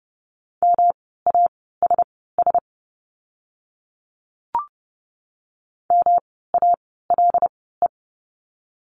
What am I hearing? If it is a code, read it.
Morse code, heard as a keyed tone, about 30 wpm, spaced as for 17 wpm, 700 Hz. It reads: MASS MALE
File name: course / 30:10wpm / 108 Lesson 18f - Letter m - Words 30wpm